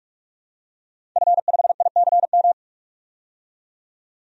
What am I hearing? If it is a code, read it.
Morse code, heard as a keyed tone, about 44 wpm, 690 Hz. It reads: F5ICM
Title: DXCC-100 Call Signs - 5 Characters - 45wpm